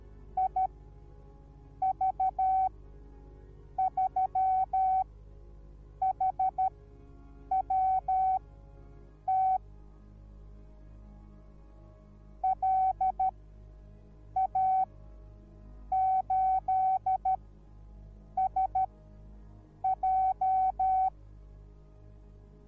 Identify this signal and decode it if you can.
Morse code, heard as a keyed tone, about 13 wpm, spaced as for 6 wpm, 760 Hz. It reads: IV3HWT LA8SJ